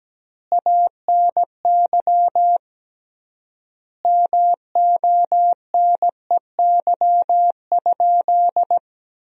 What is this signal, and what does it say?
Morse code, heard as a keyed tone, about 17 wpm, 700 Hz. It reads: ANY MONEY?
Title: 3x Word-Spacing - Sentences from Top 400 Words - 17wpm